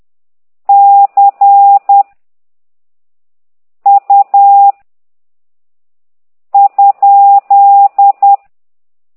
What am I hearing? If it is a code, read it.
Morse code, heard as a keyed tone, about 10 wpm, 800 Hz. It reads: C U ?
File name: ct2idw / exerc9